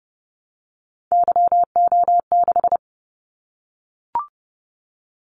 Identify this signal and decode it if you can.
Morse code, heard as a keyed tone, about 30 wpm, 700 Hz. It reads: YO6
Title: Callsign Prefixes for Top 25 DXCC - 30wpm